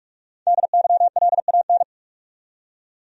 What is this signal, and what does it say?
Morse code, heard as a keyed tone, about 43 wpm, 700 Hz. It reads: DYLAN
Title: USA Common Names - Max 5 Letters - 45wpm